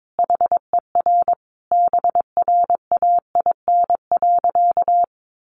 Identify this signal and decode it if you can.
Morse code, heard as a keyed tone, about 22 wpm, 700 Hz. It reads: HER BRAIN.
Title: Sentences from Top 1100 Words 22wpm